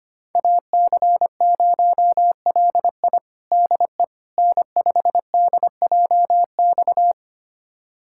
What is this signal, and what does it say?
Morse code, heard as a keyed tone, about 25 wpm, 700 Hz. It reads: AC0LI DE N5DJX
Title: USA Callsign DE Callsign 25wpm